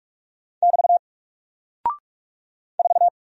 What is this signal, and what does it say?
Morse code, heard as a keyed tone, about 44 wpm, 700 Hz. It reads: <BT> 4
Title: Single Character (Letter:Number:Punctuation:Prosign) 45wpm